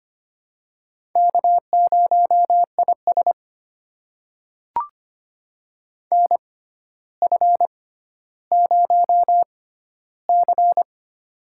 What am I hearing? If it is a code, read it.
Morse code, heard as a keyed tone, about 25 wpm, spaced as for 13 wpm, 700 Hz. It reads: K0IS NF0C
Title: USA Call Signs - 4 Characters - 6x Character Spacing - 25wpm